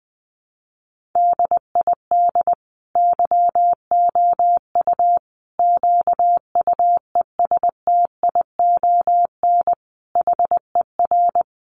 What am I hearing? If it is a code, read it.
Morse code, heard as a keyed tone, about 20 wpm, 700 Hz. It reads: DID YOU QUESTION HER
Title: Sentences from Top 800 Words 20wpm